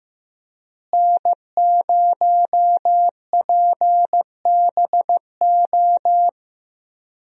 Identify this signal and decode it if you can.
Morse code, heard as a keyed tone, about 15 wpm, 700 Hz. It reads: N0PBO
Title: USA Call Signs - 5 Characters - Repeated 2x - 15wpm